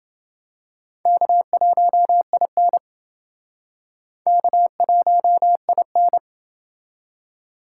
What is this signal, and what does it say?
Morse code, heard as a keyed tone, about 28 wpm, 700 Hz. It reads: K1IN K1IN